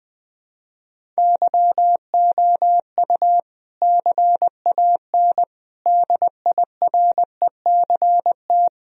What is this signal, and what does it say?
Morse code, heard as a keyed tone, about 21 wpm, 700 Hz. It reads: YOU CAN DIRECT
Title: Sentences from Top 1000 Words - Review Part 2 - 20wpm